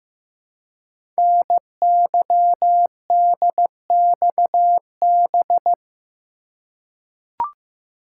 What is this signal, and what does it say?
Morse code, heard as a keyed tone, about 15 wpm, 700 Hz. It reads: NYDXB
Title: Code Groups - 5 Letters - 15wpm